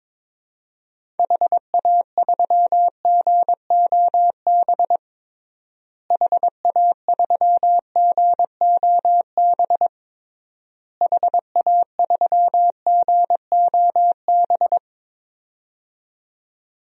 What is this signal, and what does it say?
Morse code, heard as a keyed tone, about 22 wpm, 700 Hz. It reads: HA3GOB HA3GOB HA3GOB
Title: DXCC-100 Call Signs - 6 Characters - Repeated 3x - 22wpm